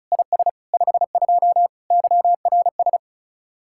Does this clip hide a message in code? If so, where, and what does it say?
Yes: Morse code from 0.1 to 3.0 s, heard as a keyed tone, about 47 wpm, 700 Hz. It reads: IS 52 YRS